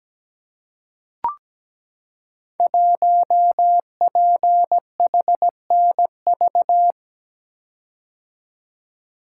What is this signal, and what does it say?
Morse code, heard as a keyed tone, about 17 wpm, 700 Hz. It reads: 1PHNV
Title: Code Groups - 5 Letter:Numbers - 17wpm